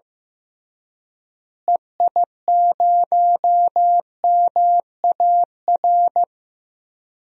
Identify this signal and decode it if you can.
Morse code, heard as a keyed tone, about 15 wpm, 700 Hz. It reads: EI0MAR
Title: DXCC-100 Call Signs - 6 Characters - Repeated 3x - 15wpm